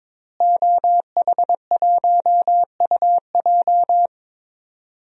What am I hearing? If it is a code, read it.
Morse code, heard as a keyed tone, about 22 wpm, 700 Hz. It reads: OH1UJ
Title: DXCC-100 Call Signs - 5 Characters - Repeated 3x - 22wpm